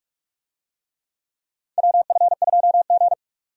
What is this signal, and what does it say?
Morse code, heard as a keyed tone, about 42 wpm, 700 Hz. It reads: WF2G
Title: USA Call Signs - 4 Characters - 45wpm